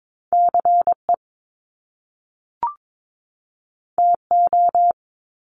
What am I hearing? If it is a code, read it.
Morse code, heard as a keyed tone, about 22 wpm, 700 Hz. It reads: CE TO
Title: Top Two Letter Combinations in Words 22wpm